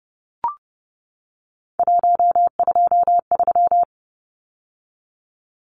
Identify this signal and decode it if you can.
Morse code, heard as a keyed tone, about 30 wpm, 700 Hz. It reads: 123
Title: Numbers - 3-Digit 30wpm